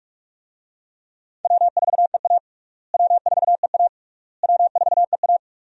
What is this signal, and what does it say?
Morse code, heard as a keyed tone, about 46 wpm, 700 Hz. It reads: W4EA W4EA W4EA
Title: USA Call Signs - 4 Characters - Repeated 3x - 45wpm